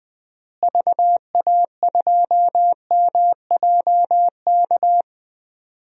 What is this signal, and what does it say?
Morse code, heard as a keyed tone, about 20 wpm, 700 Hz. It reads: VA2MJK
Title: DXCC-100 Call Signs - 6 Characters - Repeated 3x - 20wpm